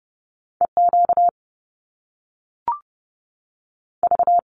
Morse code, heard as a keyed tone, about 30 wpm, 700 Hz. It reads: EQ V
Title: Top Two Letter Combinations in Words 30wpm